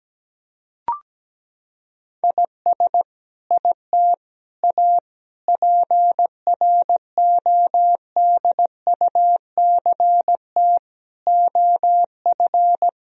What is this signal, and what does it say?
Morse code, heard as a keyed tone, about 17 wpm, 700 Hz. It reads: IS IT A PRODUCT OF
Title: Sentences from Top 1000 Words - Review Part 2 - 17wpm